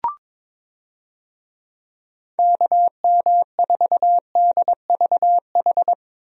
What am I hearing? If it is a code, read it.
Morse code, heard as a keyed tone, about 22 wpm, 700 Hz. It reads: KM4DVH
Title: USA Call Signs - 6 Characters - Repeated 3x - 22wpm